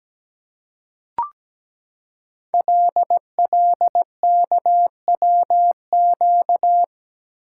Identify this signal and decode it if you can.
Morse code, heard as a keyed tone, about 17 wpm, 700 Hz. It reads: LLKWQ
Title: Code Groups - 5 Letter:Numbers - 17wpm